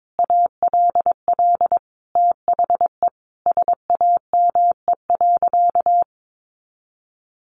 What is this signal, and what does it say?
Morse code, heard as a keyed tone, about 22 wpm, 700 Hz. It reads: ALL THE SAME.
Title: Sentences from Top 1000 Words 22wpm